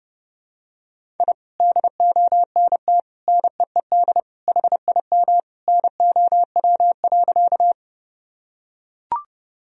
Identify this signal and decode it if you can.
Morse code, heard as a keyed tone, about 30 wpm, 700 Hz. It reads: I DONT NEED HIM NOW.
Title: Sentences from Top 1000 Words - Review Part 1 - 30wpm